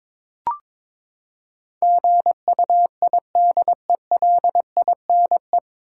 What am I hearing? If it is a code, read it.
Morse code, heard as a keyed tone, about 22 wpm, 700 Hz. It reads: GUIDELINE
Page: Morse Code Ninja practice 22wpm